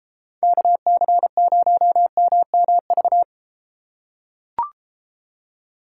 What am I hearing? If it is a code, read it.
Morse code, heard as a keyed tone, about 32 wpm, 700 Hz. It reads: KC0MMV